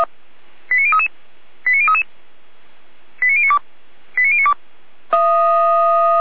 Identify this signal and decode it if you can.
Five-tone selective call (ZVEI), heard as quick runs of five tones, tones of about 70 ms. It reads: 89033 89033 89092 89092